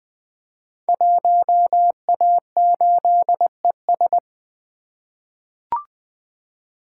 Morse code, heard as a keyed tone, about 20 wpm, 700 Hz. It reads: 1A8ES
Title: Code Groups - 5 Letter:Numbers - 20wpm